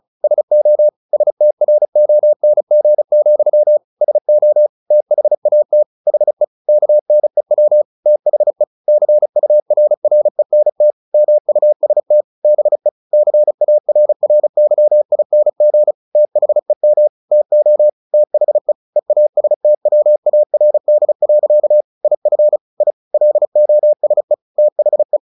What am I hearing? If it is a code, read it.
Morse code, heard as a keyed tone, about 34 wpm, 600 Hz. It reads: SO STRONG, SO THAT HE KNEW THE CURRENT MUST BE CARRYING THEM TO THE EASTWARD. IF I LOSE THE
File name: the old man and the sea / the old man and the sea 1 35wpm